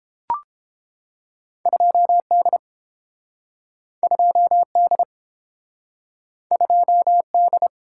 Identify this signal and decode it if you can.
Morse code, heard as a keyed tone, about 30 wpm, 700 Hz. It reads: 2D 2D 2D